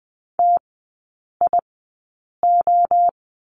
Morse code, heard as a keyed tone, about 20 wpm, spaced as for 8 wpm, 700 Hz. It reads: TIO
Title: 041 Lesson 8e - Number 1 - Three Characters 20wpm